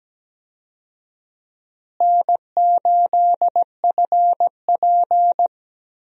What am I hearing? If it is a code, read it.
Morse code, heard as a keyed tone, about 17 wpm, 700 Hz. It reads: N8FP
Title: USA Call Signs - 4 Characters - Repeated 3x - 17wpm